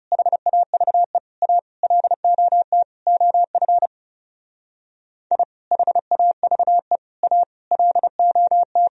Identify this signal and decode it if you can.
Morse code, heard as a keyed tone, about 36 wpm, 700 Hz. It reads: HAVE A LOT OF I HAVE A LOT
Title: Speed-Racing - Sets of 5 Words 30wpm